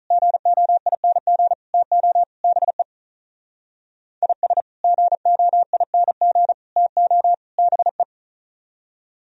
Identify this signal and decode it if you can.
Morse code, heard as a keyed tone, about 44 wpm, 700 Hz. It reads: GOING TO BE IS GOING TO BE